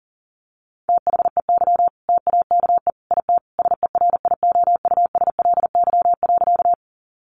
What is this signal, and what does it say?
Morse code, heard as a keyed tone, about 40 wpm, 700 Hz. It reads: THEY TAKE IT SERIOUSLY.
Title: Sentences from Top 1200 Words 40wpm